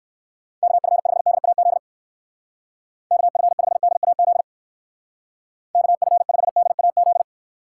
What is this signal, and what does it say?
Morse code, heard as a keyed tone, about 59 wpm, 700 Hz. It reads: KF5DAZ KF5DAZ KF5DAZ